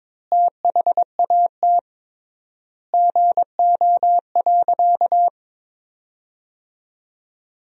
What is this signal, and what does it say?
Morse code, heard as a keyed tone, about 22 wpm, 700 Hz. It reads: THAT GO.
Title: 3x Word-Spacing - Sentences from Top 300 Words - 22wpm